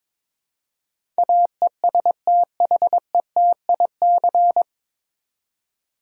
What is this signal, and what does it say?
Morse code, heard as a keyed tone, about 22 wpm, 700 Hz. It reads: AESTHETIC
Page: Morse Code Ninja practice 22wpm